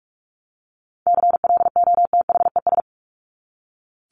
Morse code, heard as a keyed tone, about 44 wpm, 700 Hz. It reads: CLOTHES
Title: Top 1751-2000 Words 45wpm